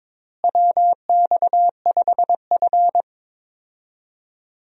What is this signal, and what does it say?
Morse code, heard as a keyed tone, about 22 wpm, 700 Hz. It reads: WX5F
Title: USA Call Signs - 4 Characters - Repeated 3x - 22wpm